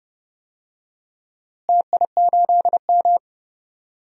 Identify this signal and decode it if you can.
Morse code, heard as a keyed tone, about 30 wpm, 700 Hz. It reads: TI8M
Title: DXCC-100 Call Signs - 4 Characters - Repeated 2x - 30wpm